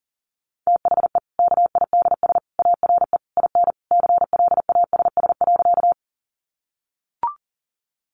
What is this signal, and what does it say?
Morse code, heard as a keyed tone, about 40 wpm, 700 Hz. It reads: THE KIDS ARE IN CLASS.
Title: Easy Phrases 40wpm